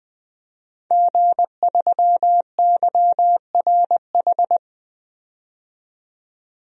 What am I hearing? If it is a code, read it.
Morse code, heard as a keyed tone, about 20 wpm, 700 Hz. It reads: G3YRH
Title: DXCC-100 Call Signs - 5 Characters - 20wpm